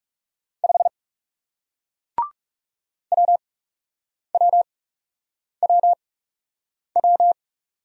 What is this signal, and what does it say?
Morse code, heard as a keyed tone, about 39 wpm, spaced as for 8 wpm, 700 Hz. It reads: 5 WWWW